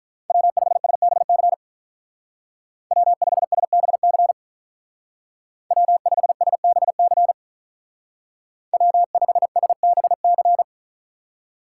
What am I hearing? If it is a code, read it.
Morse code, heard as a keyed tone, about 43 wpm, 700 Hz. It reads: W5SBC W5SBC W5SBC W5SBC
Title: Speed Racing - USA 1x3 Call Signs 35wpm